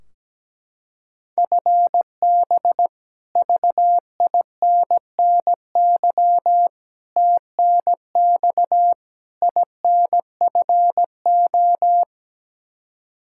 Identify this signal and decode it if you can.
Morse code, heard as a keyed tone, about 17 wpm, 700 Hz. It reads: FB VINNY TNX INFO
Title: Practice QSO - Exchange 3 - More Info - 17wpm